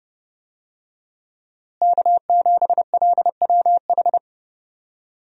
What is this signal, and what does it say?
Morse code, heard as a keyed tone, about 31 wpm, 700 Hz. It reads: K7LWH
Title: USA Call Signs - 5 Characters - Repeated 3x - 30wpm